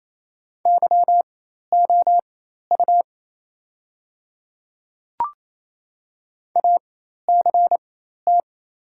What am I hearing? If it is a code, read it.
Morse code, heard as a keyed tone, about 28 wpm, spaced as for 13 wpm, 700 Hz. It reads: YOU ACT
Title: 4x Character Spacing - Top 200 Words - 28wpm